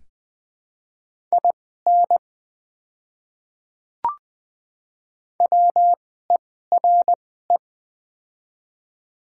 Morse code, heard as a keyed tone, about 20 wpm, spaced as for 14 wpm, 700 Hz. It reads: IN WERE
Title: 2x Character Spacing - Top 200 Words - 20wpm